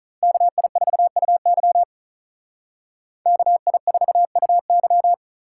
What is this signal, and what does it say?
Morse code, heard as a keyed tone, about 43 wpm, 700 Hz. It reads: KI4UY KI4UY
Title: Speed Racing - USA 2x2 Call Signs 35wpm